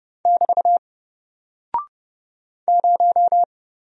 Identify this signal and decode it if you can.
Morse code, heard as a keyed tone, about 30 wpm, 700 Hz. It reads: <BT> 0
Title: Single Character (Letter:Number:Punctuation:Prosign) Alt 30wpm